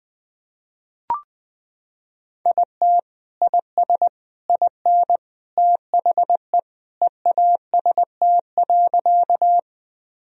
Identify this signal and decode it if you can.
Morse code, heard as a keyed tone, about 20 wpm, 700 Hz. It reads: IT IS IN THE EAST.
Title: Sentences from Top 1000 Words - Review Part 3 - 20wpm